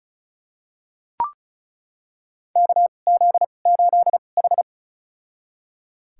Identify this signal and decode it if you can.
Morse code, heard as a keyed tone, about 36 wpm, spaced as for 25 wpm, 700 Hz. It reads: KZ8H
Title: USA Call Signs - 2x Character Spacing - 35wpm